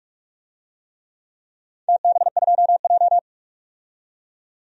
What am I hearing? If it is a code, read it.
Morse code, heard as a keyed tone, about 46 wpm, 700 Hz. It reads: TB2J
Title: DXCC-100 Call Signs - 4 Characters - 45wpm